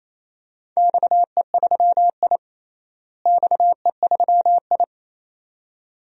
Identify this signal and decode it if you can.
Morse code, heard as a keyed tone, about 28 wpm, 700 Hz. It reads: XE3I XE3I
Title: DXCC-100 Call Signs - 4 Characters - Repeated 2x - 28wpm